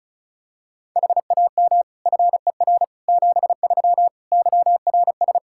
Morse code, heard as a keyed tone, about 36 wpm, 700 Hz. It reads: HAM FER 73 YRS
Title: Practice QSO - Exchange 3 - More Info - 35wpm